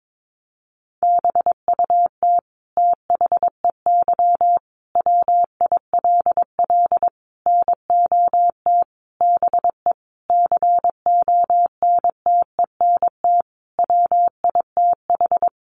Morse code, heard as a keyed tone, about 22 wpm, 700 Hz. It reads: BUT THEY WILL NOT BE CONTENT WITH
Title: Sentences from Top 900 Words 22wpm